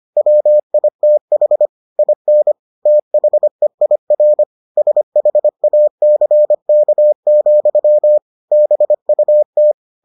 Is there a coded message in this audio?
Yes: Morse code from 0.2 to 9.7 s, heard as a keyed tone, about 25 wpm, 600 Hz. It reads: WITH IN THEIR SHACK, BUT